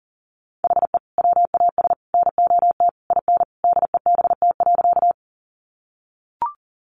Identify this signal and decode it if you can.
Morse code, heard as a keyed tone, about 40 wpm, 700 Hz. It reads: HE WAS NOT IN DEBT.
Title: Sentences from Top 1300 Words 40wpm